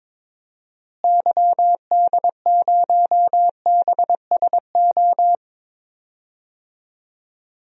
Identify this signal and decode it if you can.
Morse code, heard as a keyed tone, about 21 wpm, 700 Hz. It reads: YD0BSO
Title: DXCC-100 Call Signs - 6 Characters - Repeated 3x - 22wpm